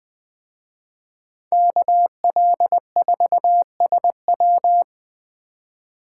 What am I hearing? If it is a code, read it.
Morse code, heard as a keyed tone, about 20 wpm, 700 Hz. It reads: KL4SW